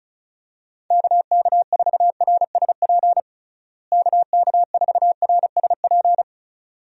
Morse code, heard as a keyed tone, about 34 wpm, 700 Hz. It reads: KK4RSP KK4RSP
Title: USA Call Signs - 6 Characters - Repeated 2x - 35wpm